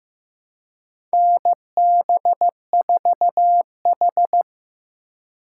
Morse code, heard as a keyed tone, about 15 wpm, 700 Hz. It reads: NB4H
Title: USA Call Signs - 4 Characters - 2x Character Spacing - 15wpm